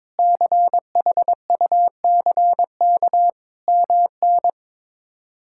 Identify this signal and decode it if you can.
Morse code, heard as a keyed tone, about 21 wpm, 700 Hz. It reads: CHUCK MN
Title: K1USN SST QSO Practice - 22wpm